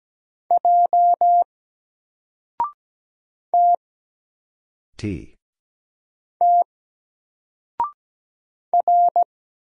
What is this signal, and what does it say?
Morse code, heard as a keyed tone, about 17 wpm, 700 Hz. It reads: J T T R
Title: Single Letter-Number 17wpm